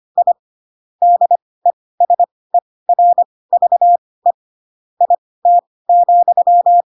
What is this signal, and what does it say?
Morse code, heard as a keyed tone, about 25 wpm, spaced as for 18 wpm, 700 Hz. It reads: I DESERVE IT,
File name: Pinocchio ch3 12 25 ik1hge